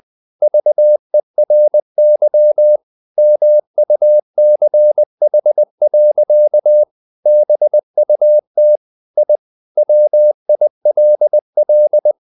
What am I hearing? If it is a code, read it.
Morse code, heard as a keyed tone, about 20 wpm, 600 Hz. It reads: VERY MUCH. BUT I WILL
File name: the old man and the sea / the old man and the sea part 1 20wpm